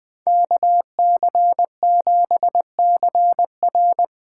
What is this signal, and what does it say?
Morse code, heard as a keyed tone, about 20 wpm, 700 Hz. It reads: KC7CR
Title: Speed Racing - USA Call Signs 17wpm